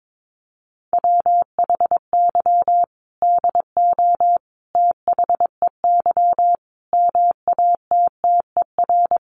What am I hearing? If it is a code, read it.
Morse code, heard as a keyed tone, about 22 wpm, 700 Hz. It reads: WHY DO THEY MATTER
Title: Sentences from Top 700 Words 22wpm